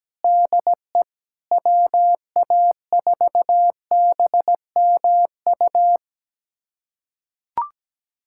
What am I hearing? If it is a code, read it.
Morse code, heard as a keyed tone, about 17 wpm, 700 Hz. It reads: DE WA4BMU
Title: USA Callsign DE Callsign - 6 Characters - 17wpm